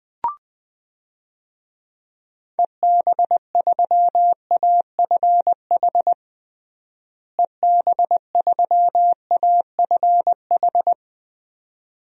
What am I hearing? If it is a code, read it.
Morse code, heard as a keyed tone, about 20 wpm, 700 Hz. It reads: EB3AFH EB3AFH
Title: DXCC-100 Call Signs - 6 Characters - Repeated 2x - 20wpm